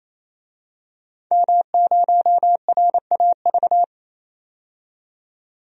Morse code, heard as a keyed tone, about 27 wpm, 700 Hz. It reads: M0RAV
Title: DXCC-100 Call Signs - 5 Characters - 28wpm